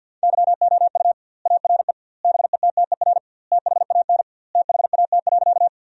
Morse code, heard as a keyed tone, about 50 wpm, 700 Hz. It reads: YOU ARE BETTER THAN THAT.
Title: 2x Word-Spacing - Sentences from Top 400 Words - 50wpm